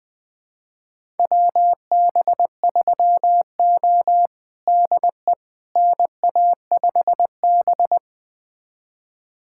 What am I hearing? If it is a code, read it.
Morse code, heard as a keyed tone, about 21 wpm, 700 Hz. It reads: WB3O DE NA5B
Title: USA Callsign DE Callsign - 4 Characters - 20wpm